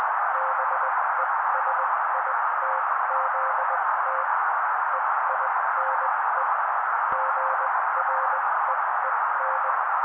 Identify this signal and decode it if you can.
Morse code, heard as a keyed tone, about 20 wpm, spaced as for 16 wpm, 530 Hz. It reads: BESITZT EINE GREEN
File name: daily / 16wpm noise